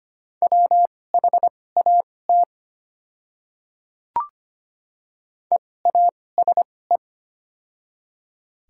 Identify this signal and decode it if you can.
Morse code, heard as a keyed tone, about 25 wpm, spaced as for 18 wpm, 700 Hz. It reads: WHAT EASE
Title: 2x Character Spacing - Top 300 Words - 25wpm